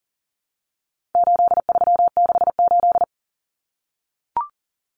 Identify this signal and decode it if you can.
Morse code, heard as a keyed tone, about 40 wpm, 700 Hz. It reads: 8368
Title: Numbers - 4-Digit 40wpm